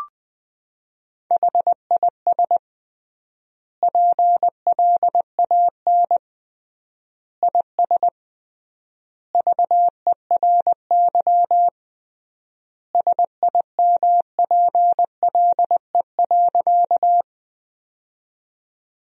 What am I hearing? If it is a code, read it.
Morse code, heard as a keyed tone, about 20 wpm, 700 Hz. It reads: HIS PLAN IS VERY SIMPLE.
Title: 3x Word-Spacing - Sentences from Top 500 Words - 20wpm